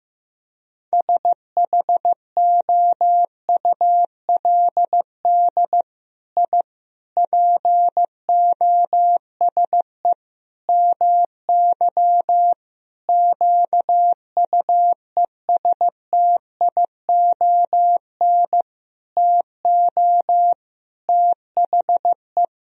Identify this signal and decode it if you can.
Morse code, heard as a keyed tone, about 15 wpm, 700 Hz. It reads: SHOULD I POSE MY QUESTION TO THE
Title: Sentences from Top 1000 Words - Review Part 2 - 15wpm